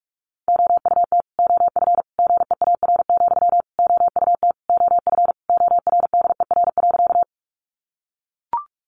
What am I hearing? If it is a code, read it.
Morse code, heard as a keyed tone, about 44 wpm, 700 Hz. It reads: OUT OF GEAR, OUT OF ORDER.
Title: Sentences from Top 1400 Words 45wpm